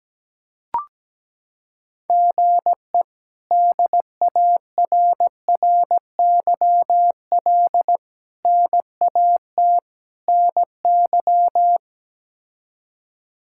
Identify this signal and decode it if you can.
Morse code, heard as a keyed tone, about 17 wpm, 700 Hz. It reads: GE DARRYL NAT NY
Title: K1USN SST QSO Practice - 17wpm